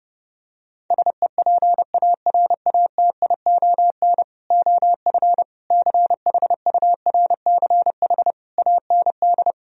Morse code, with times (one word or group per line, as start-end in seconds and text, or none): 0.9-4.2 SEPARATION
4.5-5.4 OF
5.7-8.3 CHURCH
8.6-9.5 AND